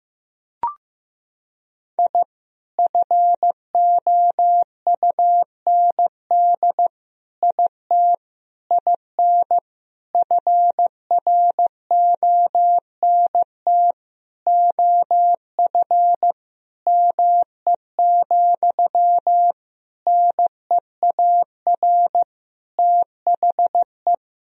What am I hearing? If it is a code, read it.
Morse code, heard as a keyed tone, about 15 wpm, 700 Hz. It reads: I FOUND IT IN FRONT OF ME, NEAR THE
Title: Sentences from Top 1000 Words - Review Part 2 - 15wpm